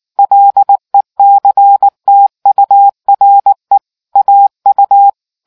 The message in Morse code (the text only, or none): LECTURE AU